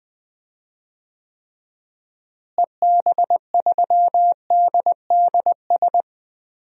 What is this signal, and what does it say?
Morse code, heard as a keyed tone, about 21 wpm, 700 Hz. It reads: EB3DDS
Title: DXCC-100 Call Signs - 6 Characters - Repeated 2x - 20wpm